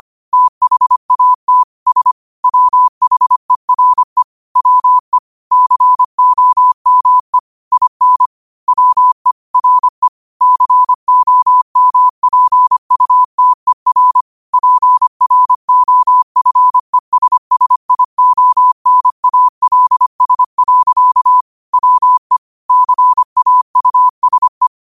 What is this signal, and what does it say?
Morse code, heard as a keyed tone, about 25 wpm, 1000 Hz. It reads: THAT S WHERE WE COME IN WE RE COMPUTER PROFESSIONALS. WE CAUSE
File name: qotdfiles / 25wpm